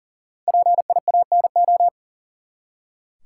Morse code, heard as a keyed tone, about 39 wpm, 700 Hz. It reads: PIANO
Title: Five Letter Words - 40wpm